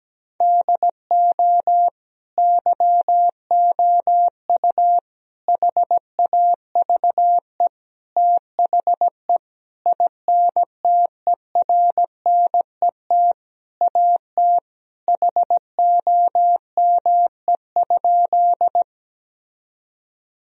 Morse code, heard as a keyed tone, about 17 wpm, 700 Hz. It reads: DO YOU HAVE THE INTERNET AT HOME?
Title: Sentences from Top 1000 Words - Review Part 4 - 17wpm